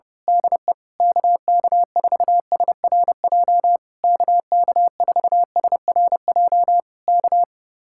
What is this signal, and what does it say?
Morse code, heard as a keyed tone, about 29 wpm, 700 Hz. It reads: DE KK4SRJ KK4SRJ K
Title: Speed-Racing Common QSO Phrases - 25wpm